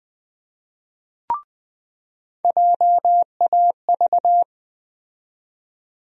Morse code, heard as a keyed tone, about 20 wpm, 700 Hz. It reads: JAV
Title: USA Call Sign Postfixes - 3 Letters 20wpm